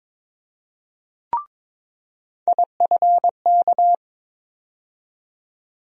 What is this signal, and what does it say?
Morse code, heard as a keyed tone, about 23 wpm, 700 Hz. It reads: IFK
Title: USA Call Sign Postfixes - 3 Letters 22wpm